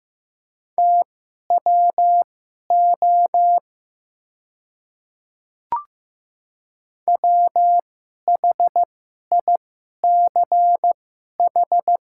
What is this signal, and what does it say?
Morse code, heard as a keyed tone, about 15 wpm, spaced as for 11 wpm, 700 Hz. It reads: TWO WHICH
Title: 2x Character Spacing - Top 200 Words - 15wpm